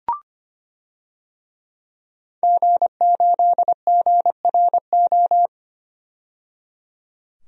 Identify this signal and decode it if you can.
Morse code, heard as a keyed tone, about 25 wpm, 700 Hz. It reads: G8GRO
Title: DXCC-100 Call Signs - 5 Characters - 25wpm